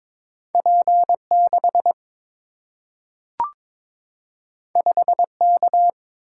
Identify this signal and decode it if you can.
Morse code, heard as a keyed tone, about 22 wpm, 700 Hz. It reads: P 6 5 K
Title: Single-Letter Single-Number 22wpm